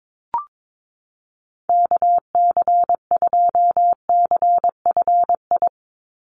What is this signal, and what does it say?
Morse code, heard as a keyed tone, about 22 wpm, 700 Hz. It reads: KC2CFI
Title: USA Call Signs 22wpm